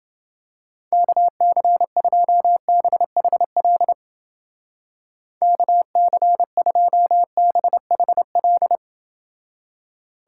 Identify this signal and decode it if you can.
Morse code, heard as a keyed tone, about 29 wpm, 700 Hz. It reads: KC2BHL KC2BHL